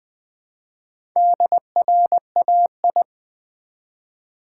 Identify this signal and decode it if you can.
Morse code, heard as a keyed tone, about 20 wpm, 700 Hz. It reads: DRAI
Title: Build A Word - 1 Letter At A Time - 3 to 5-Letter Words - 20wpm